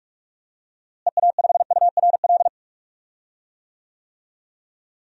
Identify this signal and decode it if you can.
Morse code, heard as a keyed tone, about 46 wpm, 700 Hz. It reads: EA5URL
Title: DXCC-100 Call Signs - 6 Characters - Repeated 2x - 45wpm